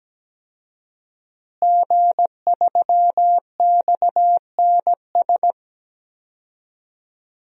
Morse code, heard as a keyed tone, about 17 wpm, 700 Hz. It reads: G3XNS
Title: DXCC-100 Call Signs - 5 Characters - Repeated 3x - 17wpm